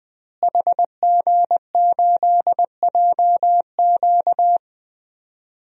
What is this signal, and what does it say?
Morse code, heard as a keyed tone, about 20 wpm, 700 Hz. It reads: HG8JQ
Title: DXCC-100 Call Signs - 5 Characters - Repeated 3x - 20wpm